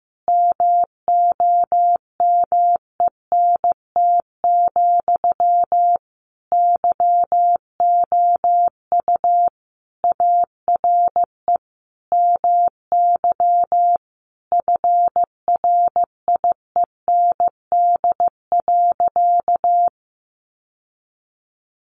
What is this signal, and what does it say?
Morse code, heard as a keyed tone, about 15 wpm, 700 Hz. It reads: MOMENT, YOU ARE MY FRIEND.
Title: Sentences from Top 1000 Words 15wpm